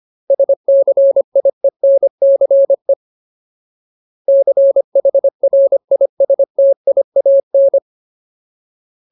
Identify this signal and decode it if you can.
Morse code, heard as a keyed tone, about 25 wpm, 550 Hz. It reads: SCIENCE CHRISTIAN
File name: Ultimate Word List / 25WPM Ultimate